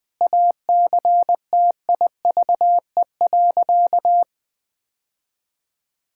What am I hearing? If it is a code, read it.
Morse code, heard as a keyed tone, about 20 wpm, 700 Hz. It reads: ACTIVE.
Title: Sentences from Top 1000 Words - Review Part 6 - 20wpm